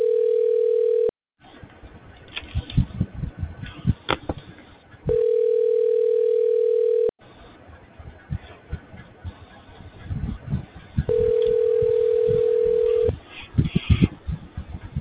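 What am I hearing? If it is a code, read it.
Ringback tone (North America), 440 + 480 Hz.
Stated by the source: Outbound business telephone call from Rick of Simply a Message.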